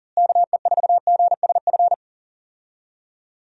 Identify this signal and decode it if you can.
Morse code, heard as a keyed tone, about 40 wpm, 700 Hz. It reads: KE4GSF
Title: USA Call Signs - 6 Characters - Repeated 2x - 40wpm